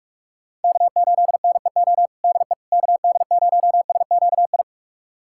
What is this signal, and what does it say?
Morse code, heard as a keyed tone, about 46 wpm, 700 Hz. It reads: K8NEO DE KD0SQI
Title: USA Callsign DE Callsign 45wpm